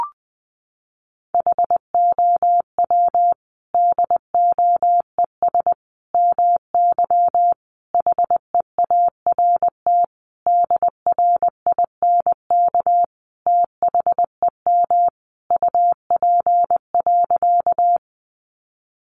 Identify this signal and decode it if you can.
Morse code, heard as a keyed tone, about 20 wpm, 700 Hz. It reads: HOW DOES MY HEART DRINK THEM UP.
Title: Sentences from Top 1500 Words 20wpm